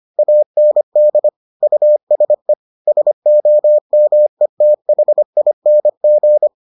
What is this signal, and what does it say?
Morse code, heard as a keyed tone, about 25 wpm, 600 Hz. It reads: AND USE SOMETHING